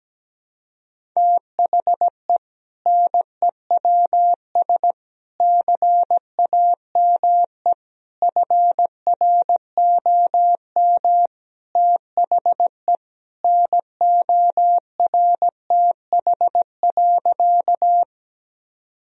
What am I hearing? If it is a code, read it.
Morse code, heard as a keyed tone, about 17 wpm, 700 Hz. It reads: THE NEWS CAME FROM THE NORTH.
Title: Sentences from Top 1000 Words - Review Part 1 - 17wpm